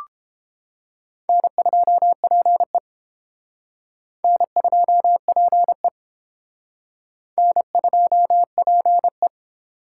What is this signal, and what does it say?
Morse code, heard as a keyed tone, about 30 wpm, 700 Hz. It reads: N2PE N2PE N2PE